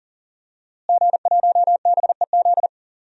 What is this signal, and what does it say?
Morse code, heard as a keyed tone, about 40 wpm, 700 Hz. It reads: G1BEZ
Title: DXCC-100 Call Signs - 5 Characters - Repeated 2x - 40wpm